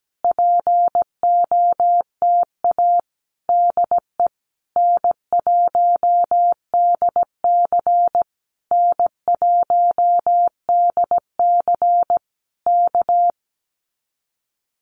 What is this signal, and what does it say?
Morse code, heard as a keyed tone, about 17 wpm, 700 Hz. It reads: POTA DE N1DC N1DC K
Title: Common QSO Phrases Encore 17wpm